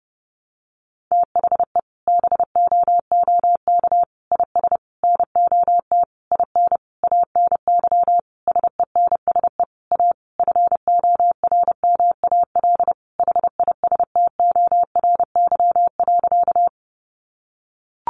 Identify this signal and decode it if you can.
Morse code, heard as a keyed tone, about 30 wpm, 700 Hz. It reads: THE BOOK IS NOT IN ANY SENSE A FORMAL HISTORY.
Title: Sentences from Top 1300 Words 30wpm